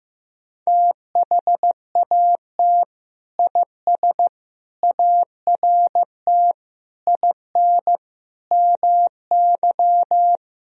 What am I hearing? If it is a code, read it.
Morse code, heard as a keyed tone, about 16 wpm, 700 Hz. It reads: THAT IS ART IN MY
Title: Sentences from Top 1000 Words - Review Part 3 - 15wpm